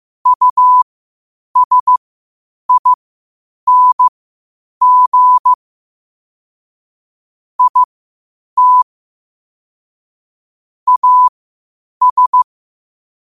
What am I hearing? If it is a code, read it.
Morse code, heard as a keyed tone, about 15 wpm, spaced as for 8 wpm, 1000 Hz. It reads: USING IT AS
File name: qotdfiles / 8wpm